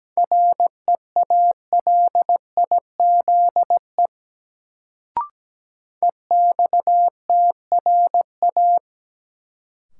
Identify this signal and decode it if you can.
Morse code, heard as a keyed tone, about 17 wpm, 700 Hz. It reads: REALIZE EXTRA
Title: Sentences from Top 1000 Words - New Word Drill - 17wpm